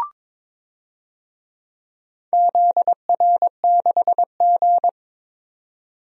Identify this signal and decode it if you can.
Morse code, heard as a keyed tone, about 22 wpm, 700 Hz. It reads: ZR6G
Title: DXCC-100 Call Signs - 4 Characters - 22wpm